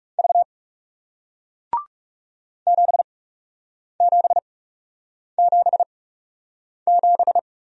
Morse code, heard as a keyed tone, about 40 wpm, spaced as for 8 wpm, 700 Hz. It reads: V 7777